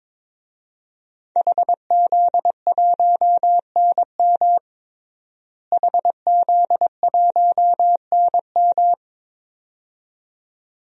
Morse code, heard as a keyed tone, about 22 wpm, 700 Hz. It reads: HZ1NM HZ1NM
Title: DXCC-100 Call Signs - 5 Characters - Repeated 2x - 22wpm